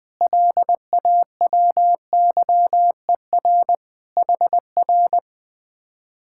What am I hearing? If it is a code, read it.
Morse code, heard as a keyed tone, about 20 wpm, 700 Hz. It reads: LAWYER HR